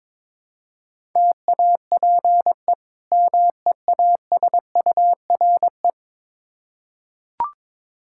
Morse code, heard as a keyed tone, about 22 wpm, 700 Hz. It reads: TAPE MEASURE
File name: practice / 22wpm / Difficult Words with Hints - 22wpm